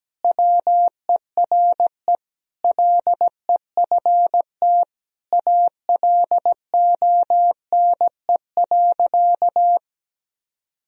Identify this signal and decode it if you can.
Morse code, heard as a keyed tone, about 17 wpm, 700 Hz. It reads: WERE LEFT ALONE.
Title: Sentences from Top 1000 Words - Review Part 1 - 17wpm